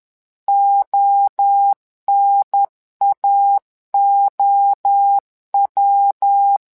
Morse code, heard as a keyed tone, about 11 wpm, 800 Hz. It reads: ONAOW
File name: LEZIONI / 3 MWRO / 3 MWRO 10wpm 2148